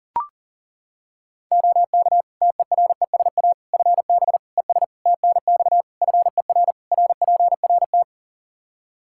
Morse code, heard as a keyed tone, about 40 wpm, 700 Hz. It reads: OK TERESA FB ES TNX FER RPRT
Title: Practice QSO - Exchange 2 - Info - 40wpm